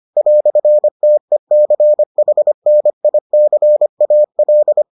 Morse code, heard as a keyed tone, about 25 wpm, 600 Hz. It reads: "TECHNICAL